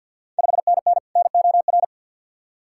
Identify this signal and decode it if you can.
Morse code, heard as a keyed tone, about 50 wpm, 700 Hz. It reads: 5NN NOR
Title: Ontario QSO Party Exchange 50wpm